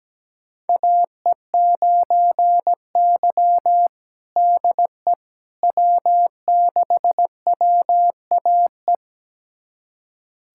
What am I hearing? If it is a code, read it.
Morse code, heard as a keyed tone, about 17 wpm, 700 Hz. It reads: AE9Y DE W6WAE
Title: USA Callsign DE Callsign 17wpm